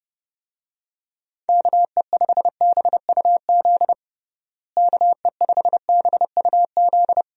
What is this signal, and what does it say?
Morse code, heard as a keyed tone, about 30 wpm, 700 Hz. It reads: KE5BUZ KE5BUZ
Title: USA Call Signs - 6 Characters - Repeated 2x - 30wpm